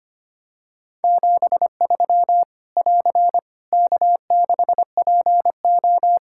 Morse code, heard as a keyed tone, about 25 wpm, 700 Hz. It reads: 73 <AR> K6PO
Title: K1USN SST QSO Practice 25wpm